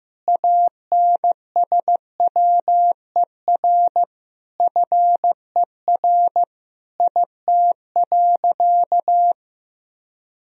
Morse code, heard as a keyed tone, about 15 wpm, 700 Hz. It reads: ANSWER FER IT.
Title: Sentences from Top 300 Words and Abbreviations 15wpm